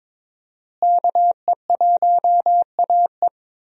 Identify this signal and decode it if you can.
Morse code, heard as a keyed tone, about 22 wpm, 700 Hz. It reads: KE1AE
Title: USA Call Signs - 5 Characters - Repeated 2x - 22wpm